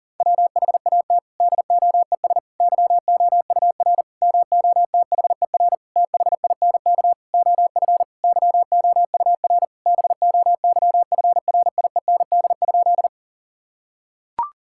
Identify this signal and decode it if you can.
Morse code, heard as a keyed tone, about 40 wpm, 700 Hz. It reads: WHAT DOES YOUR MOTHER THINK OF YOUR BOYFRIEND?
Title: Sentences from Top 1000 Words - Review Part 5 - 40wpm